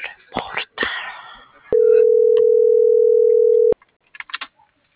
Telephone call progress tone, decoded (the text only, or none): ringback tone